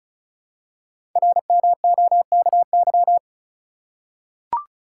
Morse code, heard as a keyed tone, about 35 wpm, 700 Hz. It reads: RMOKY
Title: Code Groups - 5 Letters - 35wpm